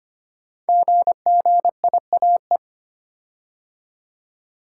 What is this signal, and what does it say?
Morse code, heard as a keyed tone, about 25 wpm, 700 Hz. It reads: GGIAE
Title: Code Groups - 5 Letters - 25wpm